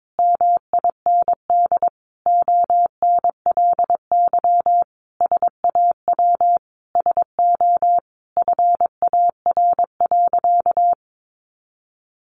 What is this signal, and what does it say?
Morse code, heard as a keyed tone, about 22 wpm, 700 Hz. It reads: MIND ONLY SAW SO FAR.
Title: Sentences from Top 800 Words 22wpm